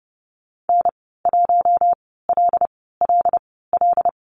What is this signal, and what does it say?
Morse code, heard as a keyed tone, about 30 wpm, spaced as for 17 wpm, 700 Hz. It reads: N1LLL
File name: Lesson 15g - Number 5 - Call Signs 30wpm